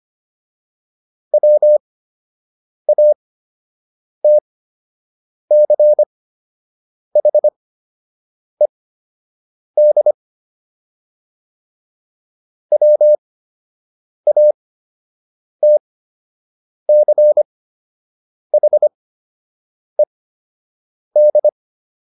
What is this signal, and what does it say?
Morse code, heard as a keyed tone, about 25 wpm, spaced as for 7 wpm, 600 Hz. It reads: WATCHED WATCHED